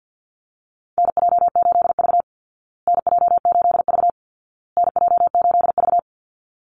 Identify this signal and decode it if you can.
Morse code, heard as a keyed tone, about 48 wpm, 700 Hz. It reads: NJ8V NJ8V NJ8V
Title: Call Signs Repeated 2x - 50wpm